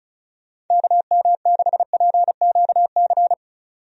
Morse code, heard as a keyed tone, about 34 wpm, 700 Hz. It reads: KM6PQC
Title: USA Call Signs - 6 Characters - 35wpm